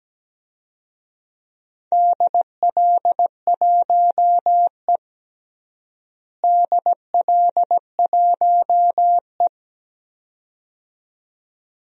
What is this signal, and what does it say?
Morse code, heard as a keyed tone, about 17 wpm, 700 Hz. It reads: DL1E DL1E
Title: DXCC-100 Call Signs - 4 Characters - Repeated 2x - 17wpm